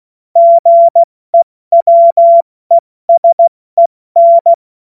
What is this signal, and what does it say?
Morse code, heard as a keyed tone, about 16 wpm, 680 Hz. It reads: GEWESEN
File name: Morsekurs / Witze / Witz 21.November 2023 70 F80